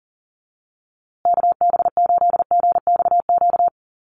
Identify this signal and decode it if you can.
Morse code, heard as a keyed tone, about 40 wpm, 700 Hz. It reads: KB8GXQ
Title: USA Call Signs Repeated 2x 40wpm